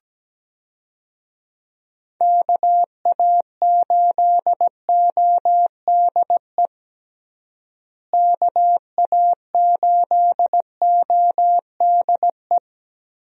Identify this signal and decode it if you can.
Morse code, heard as a keyed tone, about 17 wpm, 700 Hz. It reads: KA8ODE KA8ODE
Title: USA Call Signs - 6 Characters - Repeated 2x - 17wpm